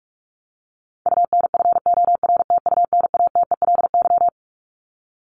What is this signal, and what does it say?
Morse code, heard as a keyed tone, about 44 wpm, 700 Hz. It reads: UNFORTUNATELY
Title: LY Suffix - Top 100 Words with Base Words 45wpm